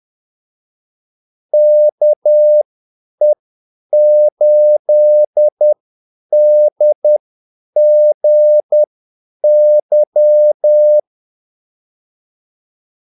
Morse code, heard as a keyed tone, about 10 wpm, spaced as for 8 wpm, 600 Hz. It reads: KE8DGY
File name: Callsigns-10-8wpm